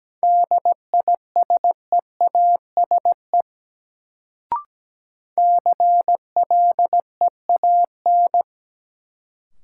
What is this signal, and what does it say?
Morse code, heard as a keyed tone, about 17 wpm, 700 Hz. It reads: DISEASE CLEAN
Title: Sentences from Top 1100 Words - New Word Drill Alt - 17wpm